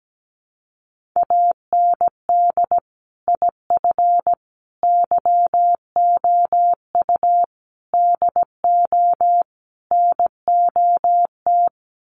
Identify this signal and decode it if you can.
Morse code, heard as a keyed tone, about 17 wpm, 700 Hz. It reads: AND IF YOU DO NOT
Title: Sets of 5 Words 17wpm